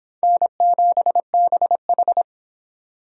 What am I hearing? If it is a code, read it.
Morse code, heard as a keyed tone, about 26 wpm, 700 Hz. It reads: N7BH